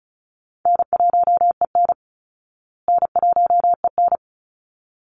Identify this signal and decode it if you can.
Morse code, heard as a keyed tone, about 34 wpm, 700 Hz. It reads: N1EN N1EN
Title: Call Signs Repeated 3x - 35wpm